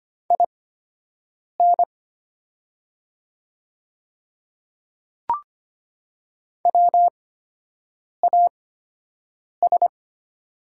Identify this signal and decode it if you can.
Morse code, heard as a keyed tone, about 25 wpm, spaced as for 7 wpm, 700 Hz. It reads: IN WAS
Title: 8x Character Spacing - Top 100 Words - 25wpm